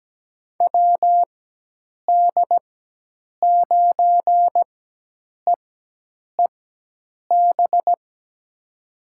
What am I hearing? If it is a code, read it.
Morse code, heard as a keyed tone, about 17 wpm, spaced as for 8 wpm, 700 Hz. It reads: WD9EEB
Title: USA Call Signs - 6 Characters - 4x Character Spacing - 17wpm